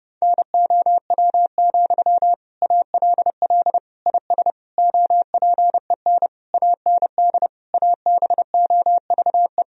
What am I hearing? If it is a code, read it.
Morse code, heard as a keyed tone, about 32 wpm, 700 Hz. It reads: NOW, ALL IS OPEN AND ABOVE
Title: Sentences from Top 1000 Words - Review Part 4 - 30wpm